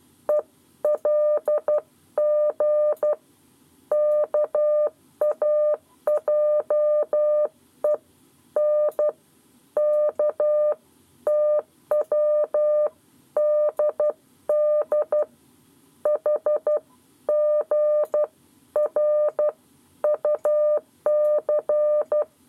Morse code, heard as a keyed tone, about 12 wpm, 600 Hz. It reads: ELG KAJE N K TWDD HGR UC
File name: 7 Lezione cw 1